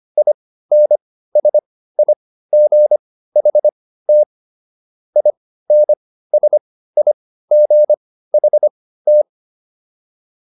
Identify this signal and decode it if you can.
Morse code, heard as a keyed tone, about 25 wpm, spaced as for 15 wpm, 600 Hz. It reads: INSIGHT INSIGHT